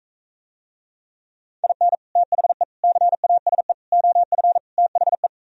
Morse code, heard as a keyed tone, about 45 wpm, 700 Hz. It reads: IN THE CASE OF THE